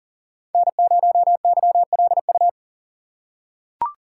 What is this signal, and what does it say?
Morse code, heard as a keyed tone, about 40 wpm, 700 Hz. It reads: N0YLU
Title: USA Call Signs - 5 Characters - 3x Character Spacing - 40wpm